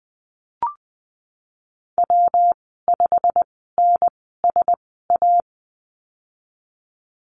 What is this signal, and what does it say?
Morse code, heard as a keyed tone, about 20 wpm, spaced as for 15 wpm, 700 Hz. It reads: W5NSA
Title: 161 Lesson 26g - Letter g - Call Signs 20wpm